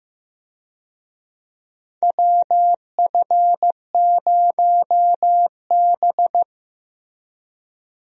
Morse code, heard as a keyed tone, about 16 wpm, 700 Hz. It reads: WF0B
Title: USA Call Signs - 4 Characters - Repeated 3x - 15wpm